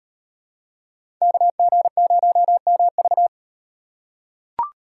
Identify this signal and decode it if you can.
Morse code, heard as a keyed tone, about 38 wpm, 700 Hz. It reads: KG0MV